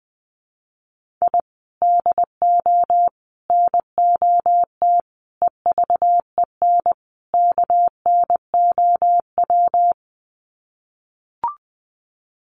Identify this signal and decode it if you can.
Morse code, heard as a keyed tone, about 20 wpm, 700 Hz. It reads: I DO NOT EVEN KNOW
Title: Sets of 5 Words 20wpm